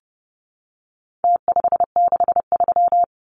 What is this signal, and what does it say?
Morse code, heard as a keyed tone, about 30 wpm, 700 Hz. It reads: T563
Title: Numbers - Contest Style - 4-Digit 30wpm